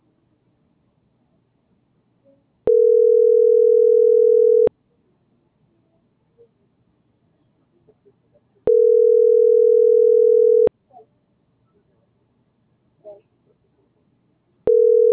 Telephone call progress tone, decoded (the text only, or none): ringback tone